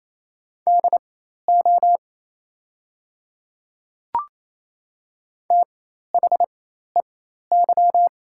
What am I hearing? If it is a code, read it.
Morse code, heard as a keyed tone, about 28 wpm, spaced as for 13 wpm, 690 Hz. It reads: DO THEY
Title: 4x Character Spacing - Top 500 Words - 28wpm